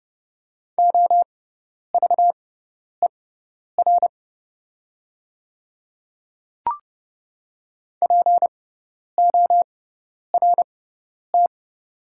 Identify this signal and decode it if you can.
Morse code, heard as a keyed tone, about 31 wpm, spaced as for 10 wpm, 700 Hz. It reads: OVER PORT
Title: 6x Character Spacing - Top 300 Words - 30wpm